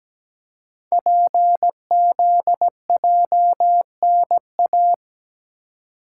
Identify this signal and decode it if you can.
Morse code, heard as a keyed tone, about 17 wpm, 700 Hz. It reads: PZJNA
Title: Code Groups - 5 Letters - 17wpm